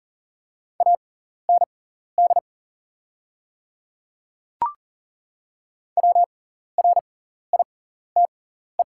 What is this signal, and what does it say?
Morse code, heard as a keyed tone, about 41 wpm, spaced as for 14 wpm, 700 Hz. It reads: AND WRITE
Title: 6x Character Spacing - Top 200 Words - 40wpm